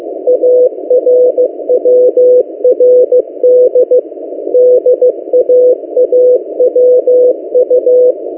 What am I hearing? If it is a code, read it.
Morse code, heard as a keyed tone, about 15 wpm, 500 Hz. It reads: ARWRD DAAWU
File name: M08a- -1355utc-8096khz